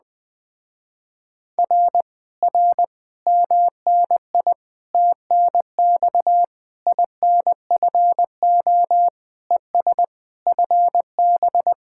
Morse code, heard as a keyed tone, about 20 wpm, 700 Hz. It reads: R R MNI TNX INFO ES FB